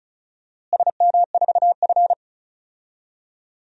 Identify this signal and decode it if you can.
Morse code, heard as a keyed tone, about 36 wpm, 700 Hz. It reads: SM4F
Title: DXCC-100 Call Signs - 4 Characters - Repeated 2x - 35wpm